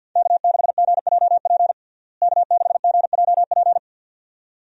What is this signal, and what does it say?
Morse code, heard as a keyed tone, about 53 wpm, 700 Hz. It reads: K6GJP K6GJP
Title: USA Call Signs - 5 Characters - Repeated 2x - 50wpm